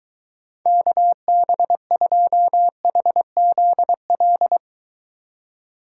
Morse code, heard as a keyed tone, about 22 wpm, 700 Hz. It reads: KB2HZL